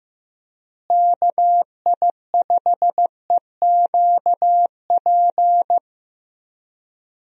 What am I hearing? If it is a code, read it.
Morse code, heard as a keyed tone, about 15 wpm, 700 Hz. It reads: KI5EQP
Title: USA Call Signs - 6 Characters - Repeated 3x - 15wpm